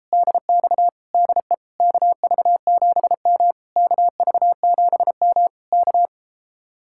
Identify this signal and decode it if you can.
Morse code, heard as a keyed tone, about 33 wpm, 700 Hz. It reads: DX DE KV7M KV7M K